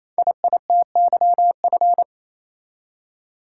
Morse code, heard as a keyed tone, about 27 wpm, 700 Hz. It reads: IITYF
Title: Code Groups - 5 Letters - 28wpm